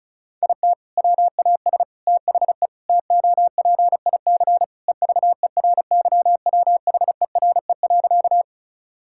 Morse code, heard as a keyed tone, about 34 wpm, 700 Hz. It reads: IT WAS THE TOPIC EVERYWHERE.
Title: Sentences from Top 1600 Words 35wpm